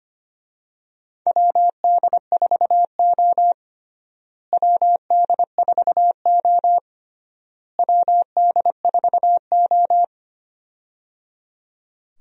Morse code, heard as a keyed tone, about 25 wpm, 700 Hz. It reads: WD4O WD4O WD4O
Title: USA Call Signs - 4 Characters - Repeated 3x - 25wpm